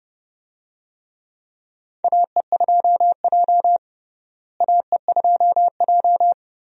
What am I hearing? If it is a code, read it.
Morse code, heard as a keyed tone, about 32 wpm, 700 Hz. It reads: AE2J AE2J
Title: USA Call Signs - 4 Characters - Repeated 3x - 30wpm